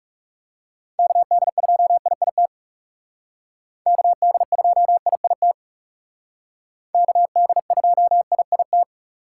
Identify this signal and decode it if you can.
Morse code, heard as a keyed tone, about 40 wpm, 700 Hz. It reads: KD2IIT KD2IIT KD2IIT